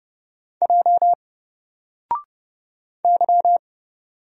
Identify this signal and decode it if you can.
Morse code, heard as a keyed tone, about 30 wpm, 700 Hz. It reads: J Y
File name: Single-Letter-Number-30wpm